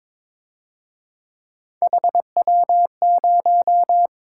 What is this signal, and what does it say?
Morse code, heard as a keyed tone, about 27 wpm, 700 Hz. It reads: HW0
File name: Speed-Racing - Three Letter-Number Combinations - 22wpm